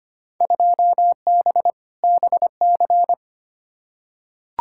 Morse code, heard as a keyed tone, about 25 wpm, 700 Hz. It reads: 2B BC
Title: Field Day Exchanges Encore - 25wpm